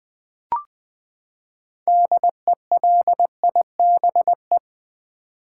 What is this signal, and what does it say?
Morse code, heard as a keyed tone, about 20 wpm, 700 Hz. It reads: DELIBE
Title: Build A Word - 6 Letters At A Time - 12-Letter Words - 20wpm